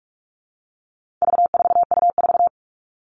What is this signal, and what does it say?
Morse code, heard as a keyed tone, about 44 wpm, 700 Hz. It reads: V4U4
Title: All Cut Numbers - 4-Digit 45wpm